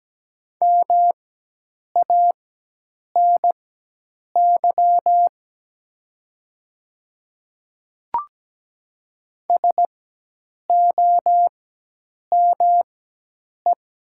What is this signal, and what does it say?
Morse code, heard as a keyed tone, about 17 wpm, spaced as for 8 wpm, 700 Hz. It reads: MANY SOME
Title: 4x Character Spacing - Top 100 Words - 17wpm